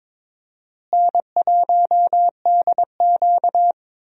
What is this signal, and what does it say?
Morse code, heard as a keyed tone, about 22 wpm, 700 Hz. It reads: N1DQ
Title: USA Call Signs - 4 Characters - Repeated 2x - 22wpm